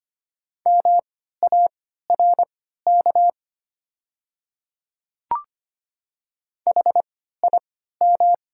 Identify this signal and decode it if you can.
Morse code, heard as a keyed tone, about 25 wpm, spaced as for 14 wpm, 700 Hz. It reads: MARK HIM
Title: 3x Character Spacing - Top 300 Words - 25wpm